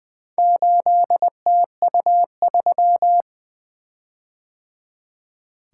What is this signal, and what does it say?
Morse code, heard as a keyed tone, about 20 wpm, 700 Hz. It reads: 8TU3
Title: Four Letter-Number Combinations - 20wpm